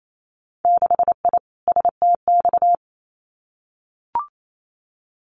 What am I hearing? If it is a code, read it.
Morse code, heard as a keyed tone, about 28 wpm, 700 Hz. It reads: 6I STX
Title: Winter Field Day Exchanges Encore 28wpm